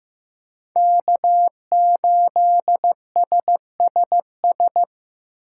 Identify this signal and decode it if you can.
Morse code, heard as a keyed tone, about 18 wpm, 700 Hz. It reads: K8SSS
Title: Speed Racing - USA 1x3 Call Signs 15wpm